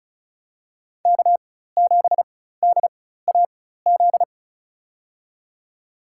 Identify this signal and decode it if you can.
Morse code, heard as a keyed tone, about 34 wpm, spaced as for 16 wpm, 700 Hz. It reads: K7DAZ
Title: USA Call Signs - 5 Characters - 4x Character Spacing - 35wpm